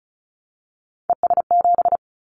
Morse code, heard as a keyed tone, about 36 wpm, 700 Hz. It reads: ES7
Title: Callsign Prefixes for Top 100 DXCC - 35wpm